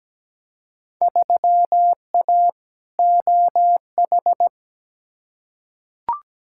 Morse code, heard as a keyed tone, about 17 wpm, 700 Hz. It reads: 3A OH
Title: Field Day Exchanges - 17wpm